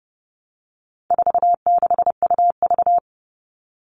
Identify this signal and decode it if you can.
Morse code, heard as a keyed tone, about 30 wpm, 700 Hz. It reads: 46UV
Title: All Cut Numbers - 4-Digit 30wpm